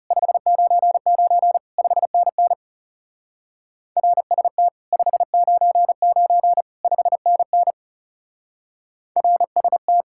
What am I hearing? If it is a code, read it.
Morse code, heard as a keyed tone, about 37 wpm, 700 Hz. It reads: 599 5NN RST 599 5NN RST
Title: Speed-Racing Common QSO Phrases Encore - 30wpm